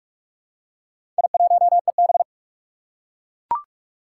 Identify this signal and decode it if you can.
Morse code, heard as a keyed tone, about 42 wpm, 700 Hz. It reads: I1EB
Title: Four Challenging Letter-Number Combinations 45wpm